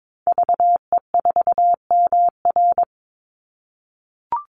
Morse code, heard as a keyed tone, about 22 wpm, 700 Hz. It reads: VE4MR
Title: Call Signs - 22wpm